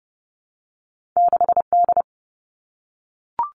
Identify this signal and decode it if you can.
Morse code, heard as a keyed tone, about 30 wpm, 700 Hz. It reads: 6D
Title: All Cut Numbers - 2-Digit 30wpm